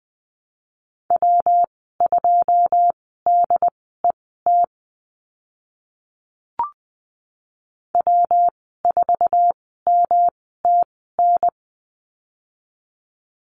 Morse code, heard as a keyed tone, about 20 wpm, spaced as for 14 wpm, 700 Hz. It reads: W2DET W4MTN
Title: 128 Lesson 21g - Number 6 - Call Signs 20wpm